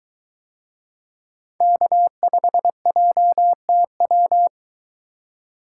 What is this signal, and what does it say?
Morse code, heard as a keyed tone, about 22 wpm, 700 Hz. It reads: K5JTW